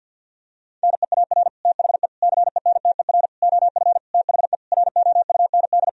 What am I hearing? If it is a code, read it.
Morse code, heard as a keyed tone, about 52 wpm, 700 Hz. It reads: NEAR THE CENTER OF THE ROUND
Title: Sentences from Top 1000 Words - Review Part 2 - 50wpm